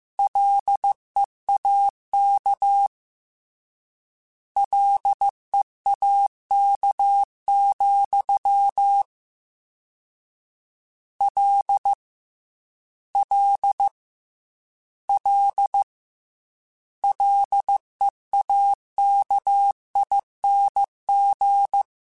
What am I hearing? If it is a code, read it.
Morse code, heard as a keyed tone, about 15 wpm, 800 Hz. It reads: LEAK LEAK, L L L LEAKING
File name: Combinations / Combinations-mp3s / Combinations@13